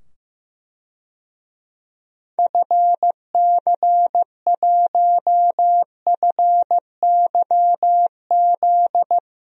Morse code, heard as a keyed tone, about 15 wpm, 700 Hz. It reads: FC1FYZ
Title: DXCC-100 Call Signs - 6 Characters - 15wpm